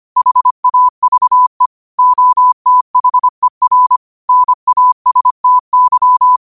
Morse code, heard as a keyed tone, about 25 wpm, 1000 Hz. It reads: HAVE OTHER NASTY
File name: qotdfiles / 25wpm